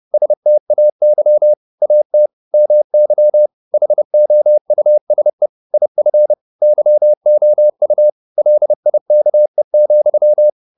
Morse code, heard as a keyed tone, about 30 wpm, 600 Hz. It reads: STAY AT MY HOUSE IF YOU LIKE,
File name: the old man and the sea / the old man and the sea 1 30wpm